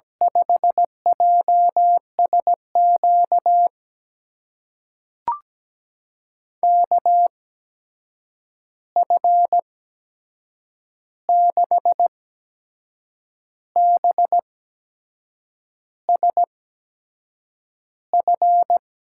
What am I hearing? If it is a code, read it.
Morse code, heard as a keyed tone, about 17 wpm, spaced as for 6 wpm, 700 Hz. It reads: 5JSQ KF6BSF
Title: USA Call Signs - 6 Characters - 8x Character Spacing - 17wpm